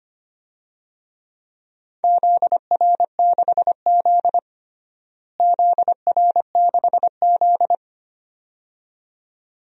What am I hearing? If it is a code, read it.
Morse code, heard as a keyed tone, about 25 wpm, 700 Hz. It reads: ZR6Z ZR6Z
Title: DXCC-100 Call Signs - 4 Characters - Repeated 2x - 25wpm